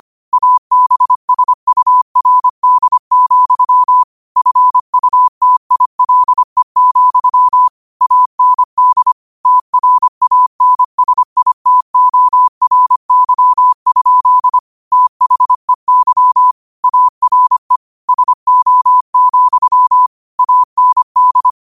Morse code, heard as a keyed tone, about 25 wpm, 1000 Hz. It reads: ABSURD, FUTILE, AND TRANSITORY? THEY ARE SO, AND
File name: qotdfiles / 25wpm